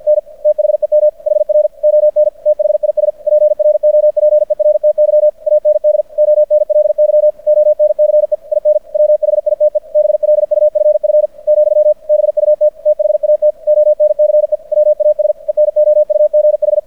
Morse code, heard as a keyed tone, about 50 wpm, 600 Hz. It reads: N THEM HR ON THIS PROPERTY AND ONLY ONCE IN WHITE BLUFF , BUT THAT ONCE WAS ENOUGH
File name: archived / CRSnet